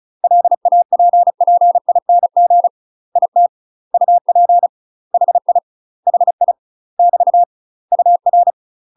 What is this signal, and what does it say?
Morse code, heard as a keyed tone, about 36 wpm, 700 Hz. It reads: LAPPING IT UP HI HI <BT> UR
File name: UK MASTER QSOs TRAINING / 35WPM / UK MASTER QSOs 0020